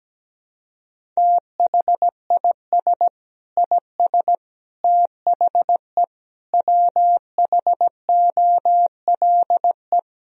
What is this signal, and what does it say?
Morse code, heard as a keyed tone, about 17 wpm, 700 Hz. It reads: THIS IS THE WHOLE
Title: Sentences from Top 1000 Words - Review Part 1 - 17wpm